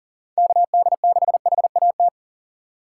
Morse code, heard as a keyed tone, about 40 wpm, 700 Hz. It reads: KD6HAT
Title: USA Call Signs - 4x Character Spacing - 40wpm